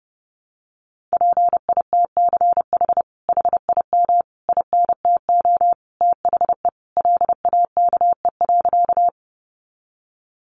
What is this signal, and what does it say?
Morse code, heard as a keyed tone, about 30 wpm, 700 Hz. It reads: PITCH HIM INTO THE LAKE.
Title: Sentences from Top 1300 Words 30wpm